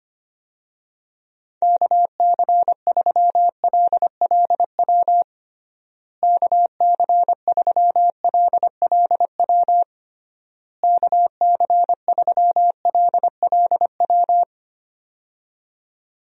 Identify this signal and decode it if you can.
Morse code, heard as a keyed tone, about 25 wpm, 700 Hz. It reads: KC3LLW KC3LLW KC3LLW
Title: USA Call Signs - 6 Characters - Repeated 3x - 25wpm